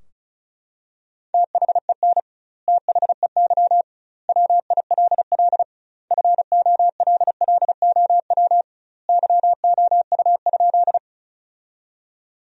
Morse code, heard as a keyed tone, about 36 wpm, 700 Hz. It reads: THEN THEY WILL FOLLOW YOU?
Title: 2x Word-Spacing - Sentences from Top 200 Words - 35wpm